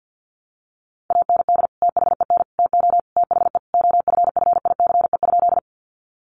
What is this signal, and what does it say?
Morse code, heard as a keyed tone, about 50 wpm, 700 Hz. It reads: AND THEN TO THE OFFICE?
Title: Sentences from Top 1000 Words 50wpm